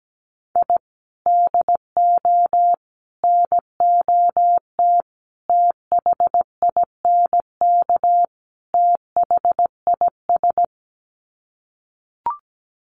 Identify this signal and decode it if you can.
Morse code, heard as a keyed tone, about 17 wpm, 690 Hz. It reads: I DO NOT THINK THIS
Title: Sets of 5 Words Encore 17wpm